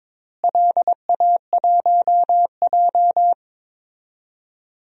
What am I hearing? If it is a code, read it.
Morse code, heard as a keyed tone, about 22 wpm, 700 Hz. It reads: LA1J
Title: DXCC-100 Call Signs - 4 Characters - Repeated 3x - 22wpm